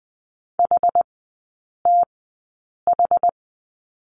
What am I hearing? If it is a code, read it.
Morse code, heard as a keyed tone, about 20 wpm, spaced as for 8 wpm, 700 Hz. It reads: HTH
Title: 060 Lesson 11e - Letter h - Three Characters 20wpm